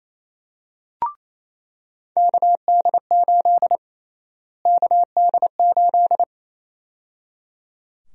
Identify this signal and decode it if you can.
Morse code, heard as a keyed tone, about 28 wpm, 700 Hz. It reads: KD8 KD8
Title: USA Call Sign Prefixes - Repeated 2x - 28wpm